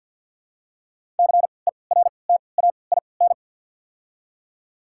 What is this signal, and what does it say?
Morse code, heard as a keyed tone, about 53 wpm, spaced as for 29 wpm, 700 Hz. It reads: CERTAIN
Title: 3x Character Spacing - Top 500 Words - 50wpm